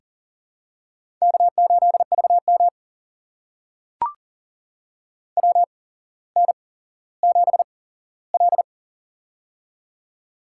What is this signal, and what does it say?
Morse code, heard as a keyed tone, about 40 wpm, spaced as for 17 wpm, 700 Hz. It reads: K8VM WN7L
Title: USA Call Signs - 4 Characters - 8x Character Spacing - 40wpm